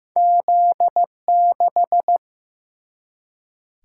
Morse code, heard as a keyed tone, about 15 wpm, 700 Hz. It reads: Z 6
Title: Single-Letter Single-Number 15wpm